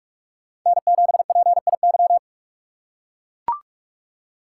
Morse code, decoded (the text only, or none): N7PIY